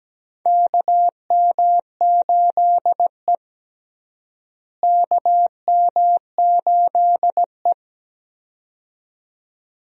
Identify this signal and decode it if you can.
Morse code, heard as a keyed tone, about 17 wpm, 700 Hz. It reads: KM8E KM8E
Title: USA Call Signs - 4 Characters - Repeated 2x - 17wpm